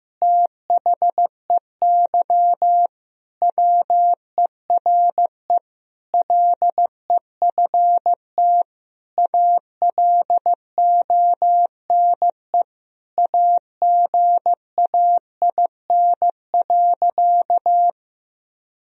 Morse code, heard as a keyed tone, about 15 wpm, 700 Hz. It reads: THEY WERE LEFT ALONE AGAIN.
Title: Sentences from Top 1000 Words - Review Part 1 - 15wpm